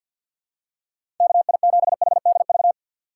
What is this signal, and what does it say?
Morse code, heard as a keyed tone, about 47 wpm, 700 Hz. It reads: KI7HDV
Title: USA Call Signs - 6 Characters - 4x Character Spacing - 50wpm